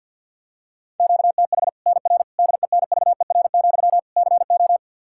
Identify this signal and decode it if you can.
Morse code, heard as a keyed tone, about 48 wpm, 700 Hz. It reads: QTH NR DENVER, CO
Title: Practice QSO - Exchange 1 - Beginning - 50wpm